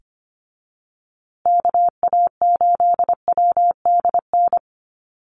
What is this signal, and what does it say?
Morse code, heard as a keyed tone, about 25 wpm, 700 Hz. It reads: KA8WDN
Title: USA Call Signs Repeated 3x 25wpm